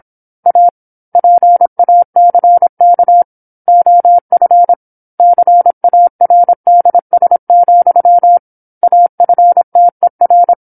Morse code, heard as a keyed tone, about 27 wpm, 700 Hz. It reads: A PACK OF CARDS, AFTER
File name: 08-Chapter-08 26 WPM